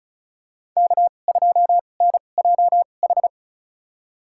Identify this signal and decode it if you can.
Morse code, heard as a keyed tone, about 34 wpm, spaced as for 25 wpm, 700 Hz. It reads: K2NJH
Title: USA Call Signs - 5 Characters - 2x Character Spacing - 35wpm